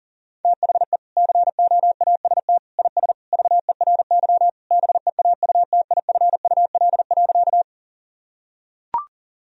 Morse code, heard as a keyed tone, about 40 wpm, 700 Hz. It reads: THE COAST IS VERY BEAUTIFUL.
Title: Sentences from Top 1000 Words - Review Part 4 - 40wpm